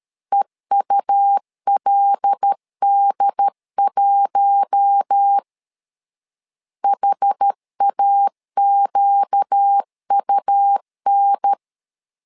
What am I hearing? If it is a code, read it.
Morse code, heard as a keyed tone, about 13 wpm, 800 Hz. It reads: EULD1 HAQUN
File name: LEZIONI / 5 LBVC / 5 LB 12wpm 1906